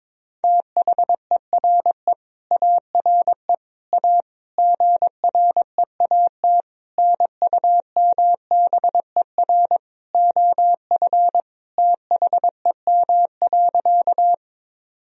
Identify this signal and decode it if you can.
Morse code, heard as a keyed tone, about 22 wpm, 700 Hz. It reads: THERE ARE A GREAT NUMBER OF THEM.
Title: Sentences from Top 1000 Words - Review Part 1 - 22wpm